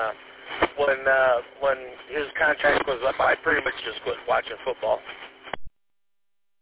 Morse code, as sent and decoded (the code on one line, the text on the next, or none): none